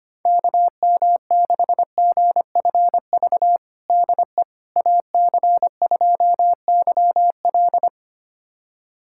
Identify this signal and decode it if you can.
Morse code, heard as a keyed tone, about 25 wpm, 700 Hz. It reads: KM6GFV DE AC2YL
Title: USA Callsign DE Callsign 25wpm